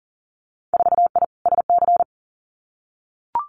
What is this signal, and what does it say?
Morse code, heard as a keyed tone, about 38 wpm, 700 Hz. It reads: 4I SC
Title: Winter Field Day Exchanges 40wpm